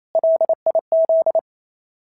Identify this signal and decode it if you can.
Morse code, heard as a keyed tone, about 27 wpm, 650 Hz. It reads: LIZ
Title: SOTA QSO Practice 28wpm